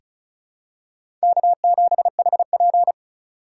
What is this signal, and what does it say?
Morse code, heard as a keyed tone, about 36 wpm, 700 Hz. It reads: K7HP
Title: USA Call Signs - 4 Characters - 4x Character Spacing - 35wpm